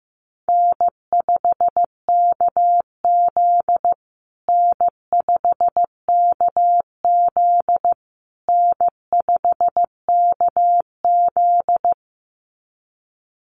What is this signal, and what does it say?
Morse code, heard as a keyed tone, about 15 wpm, 700 Hz. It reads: N5KZ N5KZ N5KZ
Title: USA Call Signs Repeated 3x 15wpm